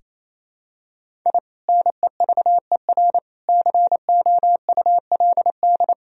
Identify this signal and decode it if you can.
Morse code, heard as a keyed tone, about 28 wpm, 700 Hz. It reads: I NEVER COULD
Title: Sentences from Top 1000 Words - Review Part 1 - 28wpm